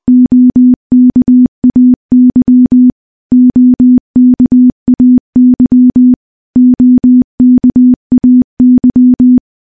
Morse code, heard as a keyed tone, about 21 wpm, 250 Hz. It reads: OKAY OKAY OKAY
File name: morse / morse-training-20WPM